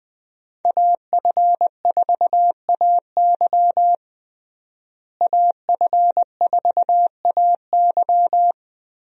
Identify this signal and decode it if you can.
Morse code, heard as a keyed tone, about 20 wpm, 700 Hz. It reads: AF4AY AF4AY
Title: USA Call Signs - 5 Characters - Repeated 3x - 20wpm